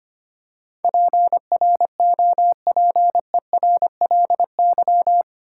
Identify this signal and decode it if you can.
Morse code, heard as a keyed tone, about 26 wpm, 700 Hz. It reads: PROPERLY
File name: practice / 25wpm / 8 Letter Words - First 3 Letters Given - 25wpm